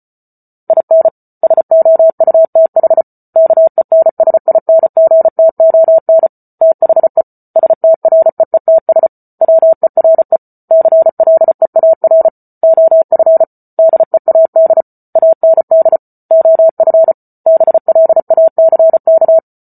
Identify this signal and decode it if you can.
Morse code, heard as a keyed tone, about 34 wpm, 650 Hz. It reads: IN SOUTH KENSINGTON THE STREETS WERE CLEAR OF DEAD AND OF BLACK